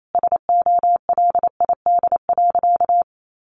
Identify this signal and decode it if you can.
Morse code, heard as a keyed tone, about 28 wpm, 700 Hz. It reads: SOLID.
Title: Sentences from Top 1500 Words 28wpm